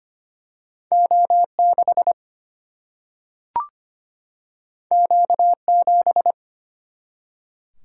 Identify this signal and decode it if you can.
Morse code, heard as a keyed tone, about 25 wpm, 700 Hz. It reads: O6 Q7
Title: Single-Letter Single-Number 25wpm